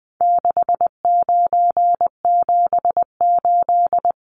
Morse code, heard as a keyed tone, about 20 wpm, 700 Hz. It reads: 6978
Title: Numbers - 4-Digit 20wpm